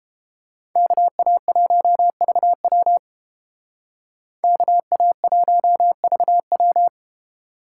Morse code, heard as a keyed tone, about 31 wpm, 700 Hz. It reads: KA1VW KA1VW